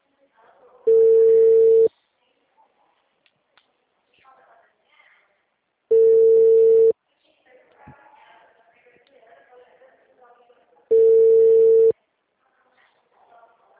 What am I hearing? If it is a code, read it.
Ringback tone (China, 450 Hz), with speech.